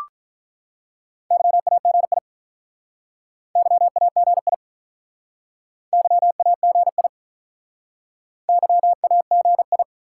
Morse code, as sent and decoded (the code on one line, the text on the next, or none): -.-- .- --. .. / -.-- .- --. .. / -.-- .- --. .. / -.-- .- --. ..
YAGI YAGI YAGI YAGI